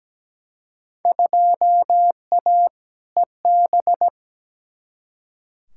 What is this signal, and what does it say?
Morse code, heard as a keyed tone, about 17 wpm, 700 Hz. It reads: 2A EB
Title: Field Day Exchanges - 17wpm